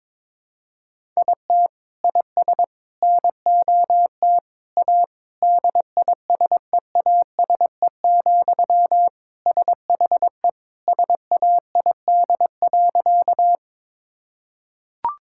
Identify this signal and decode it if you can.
Morse code, heard as a keyed tone, about 22 wpm, 700 Hz. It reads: IT IS NOT A DISEASE, SHE SAID.
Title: Sentences from Top 1000 Words - Review Part 6 - 22wpm